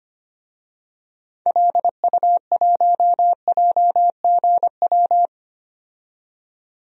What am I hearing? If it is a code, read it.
Morse code, heard as a keyed tone, about 25 wpm, 700 Hz. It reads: LU1JGW
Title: DXCC-100 Call Signs - 6 Characters - 25wpm